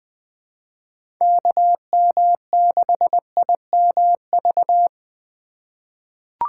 Morse code, heard as a keyed tone, about 20 wpm, 700 Hz. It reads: KM6IMV
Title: USA Call Signs - 6 Characters - 2x Character Spacing - 20wpm